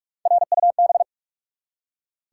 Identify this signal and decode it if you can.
Morse code, heard as a keyed tone, about 42 wpm, 700 Hz. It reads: RUB
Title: Three Letter Words - 45wpm